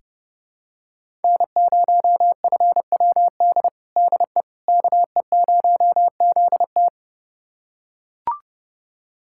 Morse code, heard as a keyed tone, about 32 wpm, 700 Hz. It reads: N0FWD DE KE0ZT
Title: USA Callsign DE Callsign - 5 Characters - 30wpm